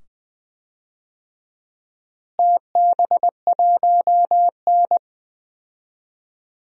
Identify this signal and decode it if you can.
Morse code, heard as a keyed tone, about 20 wpm, 700 Hz. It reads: TB1N
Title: DXCC-100 Call Signs - 4 Characters - Repeated 2x - 20wpm